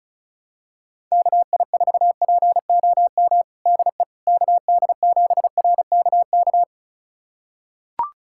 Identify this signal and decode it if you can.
Morse code, heard as a keyed tone, about 34 wpm, 700 Hz. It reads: KI4POM DE KD7RKK
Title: USA Callsign DE Callsign - 6 Characters - 35wpm